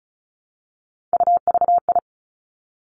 Morse code, heard as a keyed tone, about 36 wpm, 700 Hz. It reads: UVI
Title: Three Challenging Letter-Number Combinations 35wpm